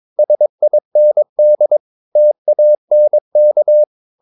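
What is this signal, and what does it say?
Morse code, heard as a keyed tone, about 22 wpm, 600 Hz. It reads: SIND TANK
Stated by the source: Morse practice of short Danish words at 22 wpm, from CW med Gnister.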